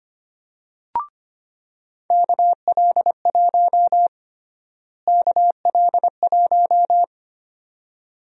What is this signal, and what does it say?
Morse code, heard as a keyed tone, about 25 wpm, 700 Hz. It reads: KL1 KL1
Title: USA Call Sign Prefixes - Repeated 2x - 25wpm